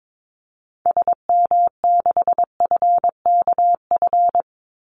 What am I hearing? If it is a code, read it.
Morse code, heard as a keyed tone, about 22 wpm, 700 Hz. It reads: SM6FKF
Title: Call Signs - 22wpm